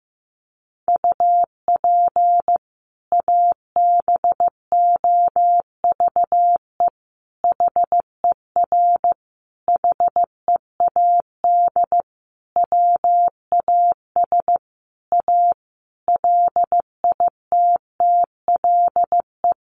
Morse code, heard as a keyed tone, about 15 wpm, 700 Hz. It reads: UP ABOVE HER HEAD WAS A LITTLE
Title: Sentences from Top 1200 Words 15wpm